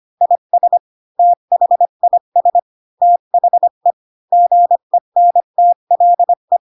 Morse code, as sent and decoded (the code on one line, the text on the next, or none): .. ... / - .... .. ... / - .... / --. -. - .-.. .
IS THIS THE GENTLE